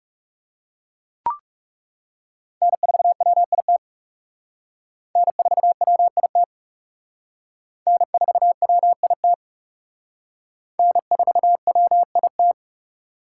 Morse code, heard as a keyed tone, about 36 wpm, 700 Hz. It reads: N4WIT N4WIT N4WIT N4WIT
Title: Speed Racing - USA Call Signs 30wpm